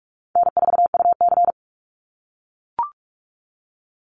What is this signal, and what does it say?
Morse code, heard as a keyed tone, about 44 wpm, 700 Hz. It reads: N4UC
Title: Call Signs - 45wpm